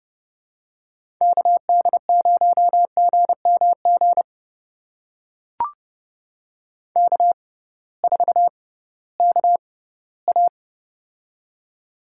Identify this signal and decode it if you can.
Morse code, heard as a keyed tone, about 30 wpm, spaced as for 18 wpm, 700 Hz. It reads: KD0GMG K4KA
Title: USA Call Signs - 6x Character Spacing - 30wpm